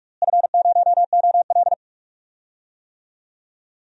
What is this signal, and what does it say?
Morse code, heard as a keyed tone, about 42 wpm, 700 Hz. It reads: F0OL
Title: DXCC-100 Call Signs - 4 Characters - Repeated 2x - 45wpm